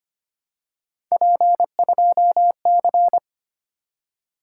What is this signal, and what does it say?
Morse code, heard as a keyed tone, about 25 wpm, 700 Hz. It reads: P2C
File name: Three Letter-Number Combinations - 25wpm